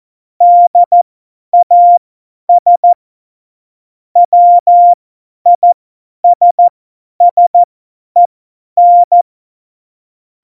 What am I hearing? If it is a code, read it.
Morse code, heard as a keyed tone, about 14 wpm, spaced as for 10 wpm, 700 Hz. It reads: DAS WISSEN
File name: Morsekurs / Witze / Witz 17 April 2018, 50 F70